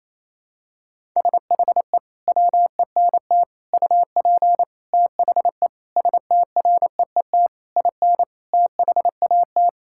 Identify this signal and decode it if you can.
Morse code, heard as a keyed tone, about 27 wpm, 700 Hz. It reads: SHE WENT UP THE STREET IN THAT
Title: Sentences from Top 1000 Words - Review Part 5 - 28wpm